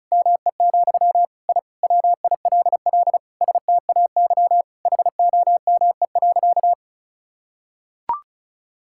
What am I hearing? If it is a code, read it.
Morse code, heard as a keyed tone, about 36 wpm, 700 Hz. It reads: ME, I WILL STAY HOME.
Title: Sentences from Top 1000 Words - Review Part 6 - 35wpm